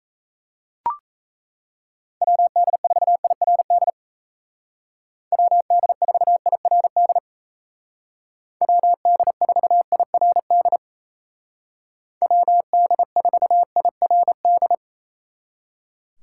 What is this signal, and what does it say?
Morse code, heard as a keyed tone, about 34 wpm, 700 Hz. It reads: WD4IRD WD4IRD WD4IRD WD4IRD
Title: Speed Racing - USA 2x3 Call Signs 28wpm